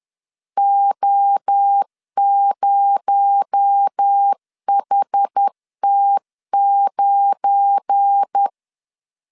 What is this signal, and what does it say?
Morse code, heard as a keyed tone, about 10 wpm, 800 Hz. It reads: O0HT9
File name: LEZIONI / 8 YJ45 / 8 Y 10wpm 2615